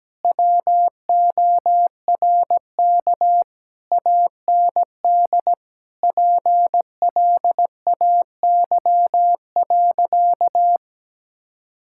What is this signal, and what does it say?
Morse code, heard as a keyed tone, about 17 wpm, 700 Hz. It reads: WORK AND PLAY.
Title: Sentences from Top 1000 Words - Review Part 6 - 17wpm